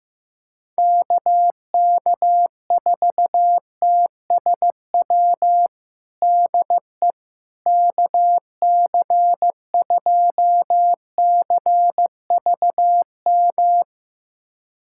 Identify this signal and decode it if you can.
Morse code, heard as a keyed tone, about 15 wpm, 700 Hz. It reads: KK4TSW DE KC2CVM
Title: USA Callsign DE Callsign - 6 Characters - 15wpm